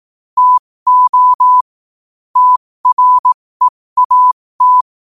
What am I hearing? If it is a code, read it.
Morse code, heard as a keyed tone, about 18 wpm, spaced as for 15 wpm, 1000 Hz. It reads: TO TREAT
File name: qotdfiles / 15wpm